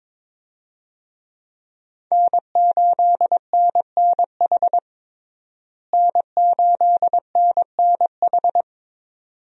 Morse code, heard as a keyed tone, about 22 wpm, 700 Hz. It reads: N8NNH N8NNH
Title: USA Call Signs - 5 Characters - Repeated 2x - 22wpm